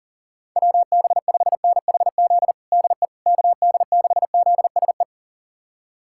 Morse code, heard as a keyed tone, about 37 wpm, 700 Hz. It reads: WB5NHZ DE KD6ZSE
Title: USA Callsign DE Callsign 40wpm